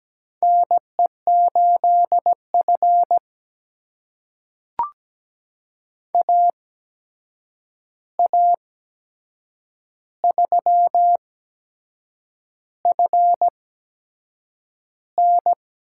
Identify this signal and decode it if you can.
Morse code, heard as a keyed tone, about 17 wpm, spaced as for 7 wpm, 700 Hz. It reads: NE8F AA3FN
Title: USA Call Signs - 8x Character Spacing - 17wpm